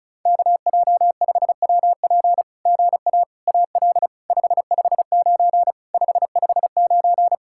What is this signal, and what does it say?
Morse code, heard as a keyed tone, about 36 wpm, 700 Hz. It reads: KJ5WP GA AL 559 559
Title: SOTA QSO Practice 35wpm